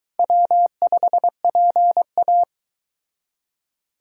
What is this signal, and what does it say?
Morse code, heard as a keyed tone, about 22 wpm, 700 Hz. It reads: W5PA